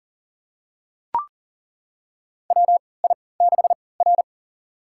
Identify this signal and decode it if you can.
Morse code, heard as a keyed tone, about 40 wpm, spaced as for 23 wpm, 700 Hz. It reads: WI6R
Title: USA Call Signs - 4 Characters - 3x Character Spacing - 40wpm